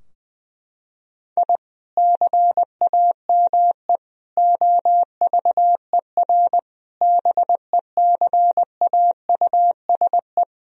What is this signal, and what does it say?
Morse code, heard as a keyed tone, about 20 wpm, 700 Hz. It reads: I CAME OVER BECAUSE
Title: Sentences from Top 1000 Words - Review Part 3 - 20wpm